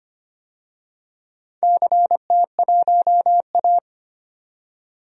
Morse code, heard as a keyed tone, about 25 wpm, 700 Hz. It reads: CT1A
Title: DXCC-100 Call Signs - 4 Characters - 25wpm